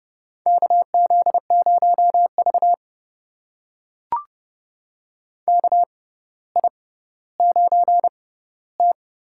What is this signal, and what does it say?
Morse code, heard as a keyed tone, about 30 wpm, spaced as for 15 wpm, 700 Hz. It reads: KZ0V KI9T
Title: USA Call Signs - 4 Characters - 6x Character Spacing - 30wpm